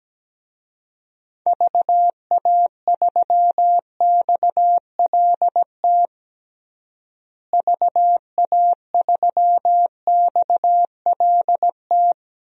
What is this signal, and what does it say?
Morse code, heard as a keyed tone, about 17 wpm, 700 Hz. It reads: VA3XLT VA3XLT
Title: DXCC-100 Call Signs - 6 Characters - Repeated 3x - 17wpm